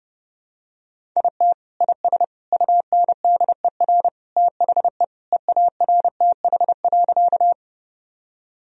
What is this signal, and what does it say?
Morse code, heard as a keyed tone, about 32 wpm, 700 Hz. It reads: IT IS UNDER THE EARTH.
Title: Sentences from Top 1000 Words - Review Part 1 - 30wpm